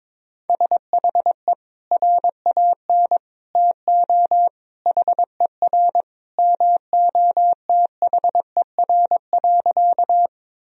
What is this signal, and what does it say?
Morse code, heard as a keyed tone, about 22 wpm, 700 Hz. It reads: SHE RAN TO HER MOTHER.
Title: Sentences from Top 1000 Words - Review Part 3 - 22wpm